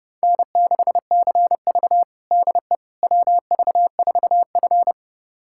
Morse code, heard as a keyed tone, about 30 wpm, 700 Hz. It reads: N6CV DE WV4F
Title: USA Callsign DE Callsign - 4 Characters - 30wpm